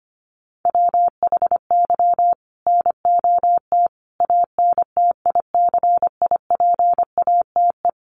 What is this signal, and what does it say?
Morse code, heard as a keyed tone, about 25 wpm, 700 Hz. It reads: WHY NOT ANTICIPATE
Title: Sentences from Top 1500 Words 25wpm